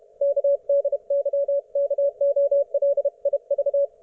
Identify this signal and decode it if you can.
Morse code, heard as a keyed tone, about 32 wpm, 560 Hz, with radio noise. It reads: KDYKOLIV